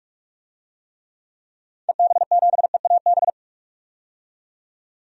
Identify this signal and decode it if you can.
Morse code, heard as a keyed tone, about 44 wpm, 700 Hz. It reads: EB7EAB
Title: DXCC-100 Call Signs - 6 Characters - 45wpm